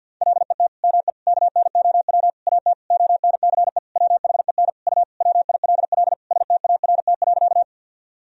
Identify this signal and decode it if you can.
Morse code, heard as a keyed tone, about 50 wpm, 700 Hz. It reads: LET ME KNOW AT ONCE WHEN U WILL START.
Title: Sentences from Top 300 Words and Abbreviations 50wpm